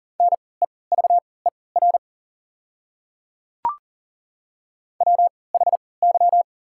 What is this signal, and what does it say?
Morse code, heard as a keyed tone, about 40 wpm, spaced as for 23 wpm, 690 Hz. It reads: NEVER WHY
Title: 3x Character Spacing - Top 300 Words - 40wpm